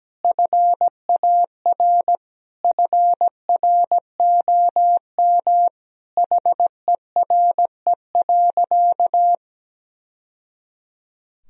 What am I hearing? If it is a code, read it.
Morse code, heard as a keyed tone, about 17 wpm, 700 Hz. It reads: FAR FROM HERE.
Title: Sentences from Top 1000 Words - Review Part 4 - 17wpm